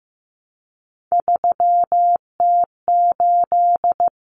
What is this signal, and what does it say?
Morse code, heard as a keyed tone, about 15 wpm, 700 Hz. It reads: 3T8
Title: Common Cut Numbers - 3-Digit 15wpm